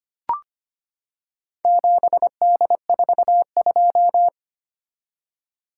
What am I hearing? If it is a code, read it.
Morse code, heard as a keyed tone, about 25 wpm, 690 Hz. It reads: 7D42
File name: practice / 25wpm / Four Letter-Number Combinations - 25wpm